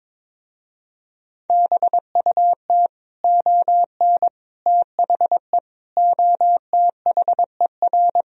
Morse code, heard as a keyed tone, about 23 wpm, 700 Hz. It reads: BUT ON THE OTHER
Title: Speed-Racing - Sets of 5 Words 22wpm